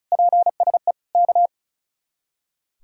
Morse code, heard as a keyed tone, about 45 wpm, 700 Hz. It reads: PSE K